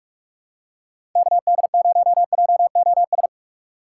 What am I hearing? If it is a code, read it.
Morse code, heard as a keyed tone, about 44 wpm, 700 Hz. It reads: KD0JOS
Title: USA Call Signs - 4x Character Spacing - 45wpm